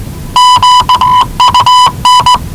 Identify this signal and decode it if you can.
Morse code, heard as a keyed tone, about 19 wpm, 990 Hz. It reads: QUN
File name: WellingtonRadioZlwCloseDownOnHfAndMf / 01-Zlw 4277khz 417.5 qrt until 0925z